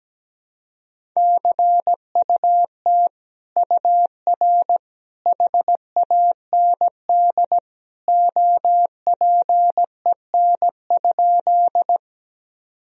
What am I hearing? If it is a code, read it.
Morse code, heard as a keyed tone, about 17 wpm, 700 Hz. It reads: CUT UR HAND OPEN?
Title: Sentences from Top 300 Words and Abbreviations 17wpm